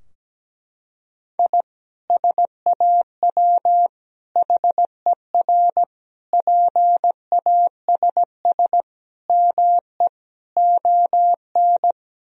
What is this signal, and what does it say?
Morse code, heard as a keyed tone, about 17 wpm, 700 Hz. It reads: I SAW HER PASS ME ON
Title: Sentences from Top 1000 Words - Review Part 2 - 17wpm